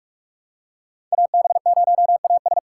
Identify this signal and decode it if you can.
Morse code, heard as a keyed tone, about 46 wpm, 700 Hz. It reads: AB0AS
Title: USA Call Signs - 3x Character Spacing - 45wpm